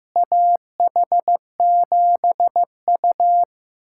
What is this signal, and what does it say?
Morse code, heard as a keyed tone, about 15 wpm, 700 Hz. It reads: AH7U
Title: USA Callsign DE Callsign - 4 Characters - 15wpm